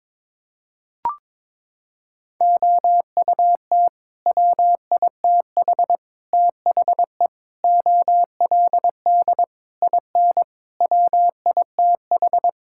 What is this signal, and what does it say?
Morse code, heard as a keyed tone, about 22 wpm, 700 Hz. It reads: OUT WITH THE OLD IN WITH
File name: practice / 22wpm / Sets of 8 Words - 22wpm